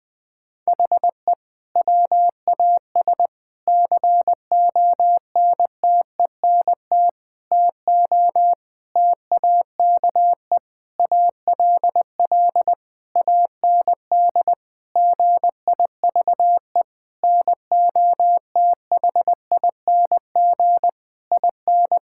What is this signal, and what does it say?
Morse code, heard as a keyed tone, about 20 wpm, 700 Hz. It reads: HE WAS CONTENT TO TAKE ALL AND GIVE NOTHING IN
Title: Sentences from Top 1000 Words - Review Part 6 - 20wpm